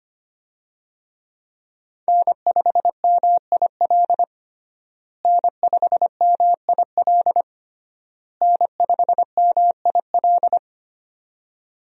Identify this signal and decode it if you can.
Morse code, heard as a keyed tone, about 25 wpm, 700 Hz. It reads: N5MIL N5MIL N5MIL
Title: USA Call Signs - 5 Characters - Repeated 3x - 25wpm